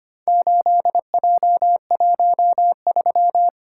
Morse code, heard as a keyed tone, about 25 wpm, 700 Hz. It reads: 8J13
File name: practice / 25wpm / Four Letter-Number Combinations - 25wpm